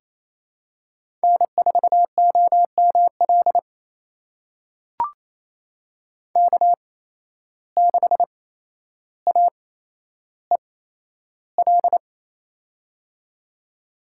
Morse code, heard as a keyed tone, about 28 wpm, spaced as for 12 wpm, 700 Hz. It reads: N4OML K6AEL
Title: USA Call Signs - 5 Characters - 8x Character Spacing - 28wpm